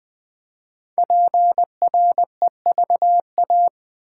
Morse code, heard as a keyed tone, about 20 wpm, 700 Hz. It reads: PREVA
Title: Build A Word - 5 Letters At A Time - 10-Letter Words - 20wpm